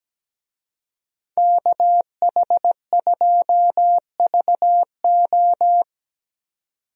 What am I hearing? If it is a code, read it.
Morse code, heard as a keyed tone, about 17 wpm, 700 Hz. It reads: KH2VO
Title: USA Call Signs - 5 Characters - 17wpm